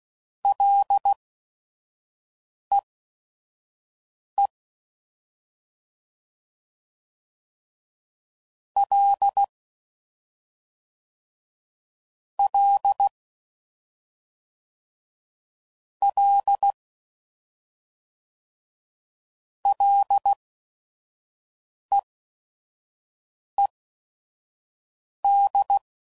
Morse code, heard as a keyed tone, about 16 wpm, spaced as for 5 wpm, 800 Hz. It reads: LEE L L L LEED
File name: Prefixes / FLs & THs / Single Fs Ls